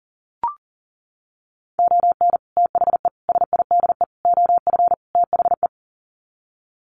Morse code, heard as a keyed tone, about 40 wpm, 700 Hz. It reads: ON THE SIDE OF THE
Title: Sets of 5 Words 40wpm